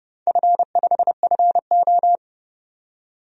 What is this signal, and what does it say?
Morse code, heard as a keyed tone, about 30 wpm, 700 Hz. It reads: F5FO
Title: DXCC-100 Call Signs - 4 Characters - 30wpm